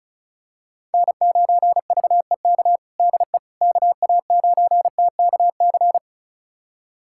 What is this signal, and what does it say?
Morse code, heard as a keyed tone, about 36 wpm, 700 Hz. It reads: N9VEK DE KA9TKC
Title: USA Callsign DE Callsign 35wpm